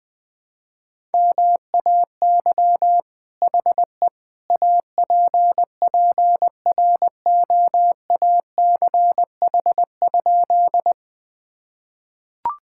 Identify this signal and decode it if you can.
Morse code, heard as a keyed tone, about 20 wpm, 700 Hz. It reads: MAY HE APPROACH?
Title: Sentences from Top 1000 Words - Review Part 6 - 20wpm